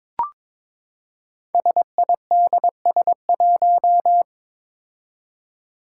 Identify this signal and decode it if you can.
Morse code, heard as a keyed tone, about 22 wpm, 700 Hz. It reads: SIDS1
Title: Five Challenging Letter-Number Combinations 22wpm